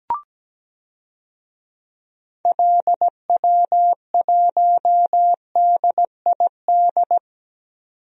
Morse code, heard as a keyed tone, about 17 wpm, 700 Hz. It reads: LW1DID
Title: DXCC-100 Call Signs - 6 Characters - 17wpm